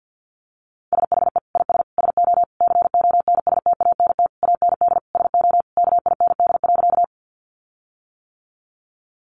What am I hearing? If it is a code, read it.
Morse code, heard as a keyed tone, about 50 wpm, 700 Hz. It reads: SHE IS SO CONSTANT AND SO KIND.
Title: Sentences from Top 1500 Words 50wpm